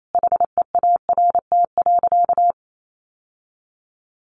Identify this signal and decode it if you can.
Morse code, heard as a keyed tone, about 28 wpm, 700 Hz. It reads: HEART.
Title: Sentences from Top 1200 Words 28wpm